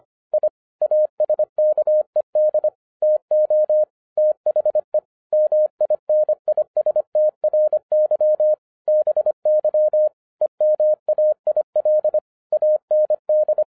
Morse code, heard as a keyed tone, about 25 wpm, 610 Hz. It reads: I ASKED TO THE MINISTRY BY EMAIL AND